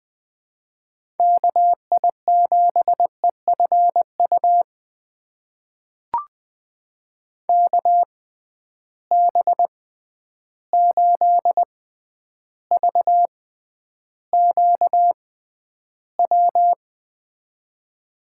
Morse code, heard as a keyed tone, about 20 wpm, spaced as for 10 wpm, 700 Hz. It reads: KI7EFU KB8VQW
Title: USA Call Signs - 6 Characters - 6x Character Spacing - 20wpm